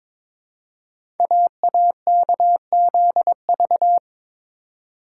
Morse code, heard as a keyed tone, about 22 wpm, 690 Hz. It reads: AAKZV
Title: Code Groups - 5 Letter:Numbers - 22wpm